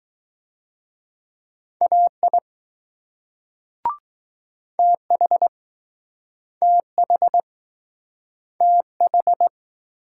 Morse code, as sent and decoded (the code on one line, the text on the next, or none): .- .. / - .... / - .... / - ....
AI TH TH TH